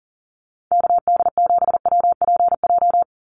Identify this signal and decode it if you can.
Morse code, heard as a keyed tone, about 40 wpm, 700 Hz. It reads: KD7WPJ
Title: NA SOTA Activator Callsigns 40wpm